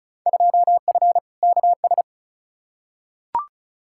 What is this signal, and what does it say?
Morse code, heard as a keyed tone, about 36 wpm, 700 Hz. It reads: 2F KS
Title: Field Day Exchanges Encore - 35wpm